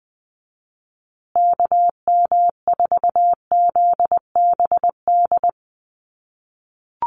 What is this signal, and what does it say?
Morse code, heard as a keyed tone, about 20 wpm, 700 Hz. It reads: KM4ZBD
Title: USA Call Signs Repeated 3x 20wpm